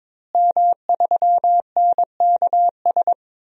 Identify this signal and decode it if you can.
Morse code, heard as a keyed tone, about 22 wpm, 700 Hz. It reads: M3NKS
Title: DXCC-100 Call Signs - 5 Characters - Repeated 3x - 22wpm